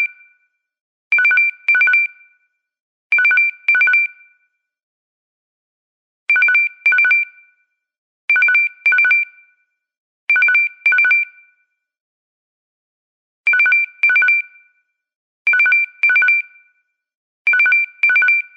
There is no code.